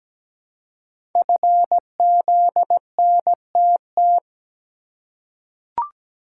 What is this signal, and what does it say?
Morse code, heard as a keyed tone, about 17 wpm, 700 Hz. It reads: FZNTT
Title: Code Groups - 5 Letters - 17wpm